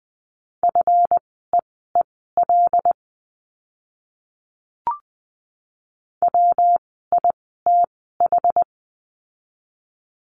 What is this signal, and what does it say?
Morse code, heard as a keyed tone, about 20 wpm, spaced as for 14 wpm, 700 Hz. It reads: FEEL WITH
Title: 139 Lesson 23f - Letter f - Words 20wpm